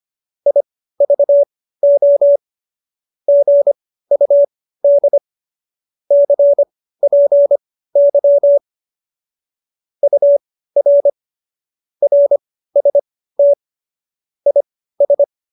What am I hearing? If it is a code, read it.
Morse code, heard as a keyed tone, about 25 wpm, spaced as for 15 wpm, 570 Hz. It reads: IVO GUD CPY UR RST IS
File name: qso219b 15